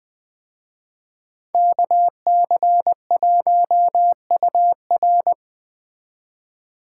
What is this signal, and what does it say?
Morse code, heard as a keyed tone, about 20 wpm, 700 Hz. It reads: KC1UR